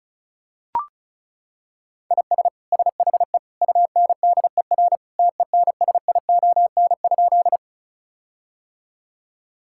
Morse code, heard as a keyed tone, about 36 wpm, 700 Hz. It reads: IS SHE UNDER TENSION?
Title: Sentences from Top 1600 Words 35wpm